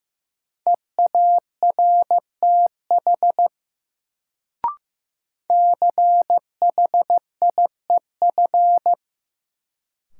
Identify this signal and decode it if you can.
Morse code, heard as a keyed tone, about 15 wpm, 700 Hz. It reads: EARTH CHIEF
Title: Sets of 5 Words Encore - New Word Drill - 15wpm